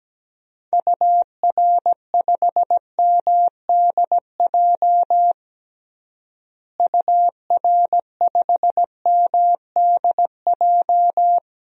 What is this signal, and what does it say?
Morse code, heard as a keyed tone, about 17 wpm, 700 Hz. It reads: UR5MDJ UR5MDJ
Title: DXCC-100 Call Signs - 6 Characters - Repeated 3x - 17wpm